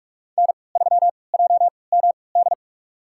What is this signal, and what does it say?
Morse code, heard as a keyed tone, about 45 wpm, spaced as for 25 wpm, 700 Hz. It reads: N3JMD
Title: USA Call Signs - 5 Characters - 3x Character Spacing - 45wpm